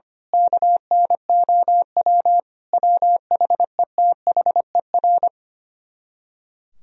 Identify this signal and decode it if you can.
Morse code, heard as a keyed tone, about 24 wpm, 700 Hz. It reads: KNOW WHETHER
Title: Speed-Racing - Sets of 5 Words 25wpm